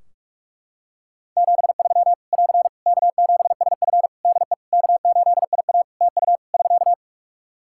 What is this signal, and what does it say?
Morse code, heard as a keyed tone, about 41 wpm, 700 Hz. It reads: 73 <AR> K7SF DE K8IA TU <SK>
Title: Practice QSO - Exchange 4 - Ending - 45wpm